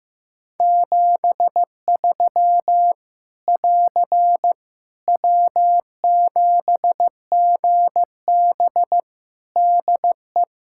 Morse code, heard as a keyed tone, about 15 wpm, 700 Hz. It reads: 73 <AR> W7GB DE
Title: Practice QSO - Exchange 4 - Ending - 15wpm